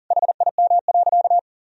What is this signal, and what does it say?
Morse code, heard as a keyed tone, about 40 wpm, 700 Hz. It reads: HIM.
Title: Sentences from Top 1000 Words - Review Part 3 - 40wpm